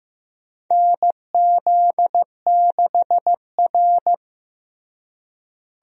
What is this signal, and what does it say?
Morse code, heard as a keyed tone, about 15 wpm, 700 Hz. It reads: NZ6R
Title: USA Call Signs - 4 Characters - Repeated 2x - 15wpm